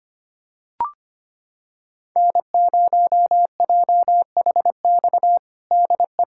Morse code, heard as a keyed tone, about 25 wpm, 700 Hz. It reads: N0JHX DE
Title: USA Callsign DE Callsign - 5 Characters - 25wpm